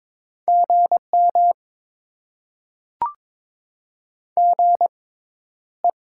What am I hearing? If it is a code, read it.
Morse code, heard as a keyed tone, about 22 wpm, spaced as for 11 wpm, 700 Hz. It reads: GM GE
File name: Top 20 QSO Elements - 6x Character Spacing - 22wpm